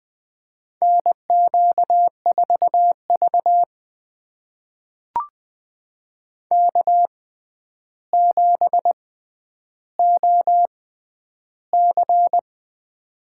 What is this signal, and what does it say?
Morse code, heard as a keyed tone, about 21 wpm, spaced as for 10 wpm, 700 Hz. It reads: NQ4V K7OC
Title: USA Call Signs - 4 Characters - 6x Character Spacing - 20wpm